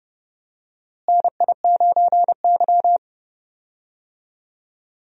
Morse code, heard as a keyed tone, about 30 wpm, 700 Hz. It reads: NI9Y
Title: USA Call Signs - 4 Characters - Repeated 3x - 30wpm